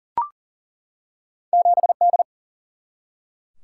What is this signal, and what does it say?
Morse code, heard as a keyed tone, about 38 wpm, 700 Hz. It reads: 7D
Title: Single-Letter Single-Number 40wpm